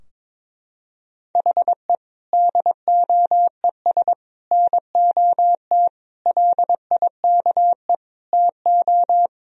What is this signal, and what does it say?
Morse code, heard as a keyed tone, about 22 wpm, 700 Hz. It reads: HE DOES NOT LIKE TO
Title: Sentences from Top 1000 Words - Review Part 3 - 22wpm